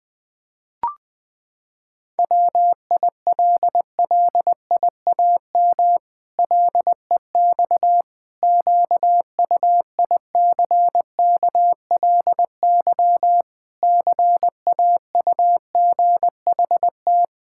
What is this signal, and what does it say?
Morse code, heard as a keyed tone, about 20 wpm, 700 Hz. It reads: WILLIAM LEX QUICKLY CAUGHT
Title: Pangram Challenge v2 - 20wpm